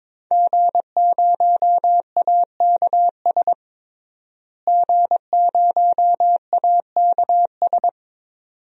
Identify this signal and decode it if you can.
Morse code, heard as a keyed tone, about 22 wpm, 700 Hz. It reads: G0AKS G0AKS
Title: DXCC-100 Call Signs - 5 Characters - Repeated 2x - 22wpm